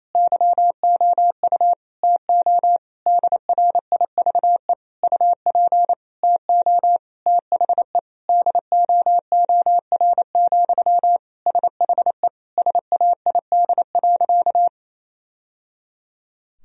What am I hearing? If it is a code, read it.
Morse code, heard as a keyed tone, about 28 wpm, 700 Hz. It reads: YOU TO DRIVE UP TO THE DOOR, SHE SAID.
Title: Sentences from Top 1000 Words - Review Part 2 - 28wpm